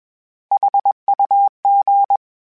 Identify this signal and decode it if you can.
Morse code, heard as a keyed tone, about 21 wpm, 800 Hz. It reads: HUG
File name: H Prefixes / H mp3s